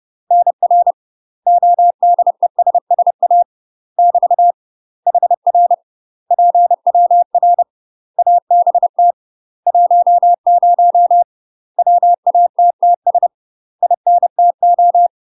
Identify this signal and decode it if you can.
Morse code, heard as a keyed tone, about 30 wpm, 700 Hz. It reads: NR ODESSA <BT> HR PWR ABT 10 WATTS INTO
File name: UK MASTER QSOs TRAINING / 30WPM / UK MASTER QSOs 0039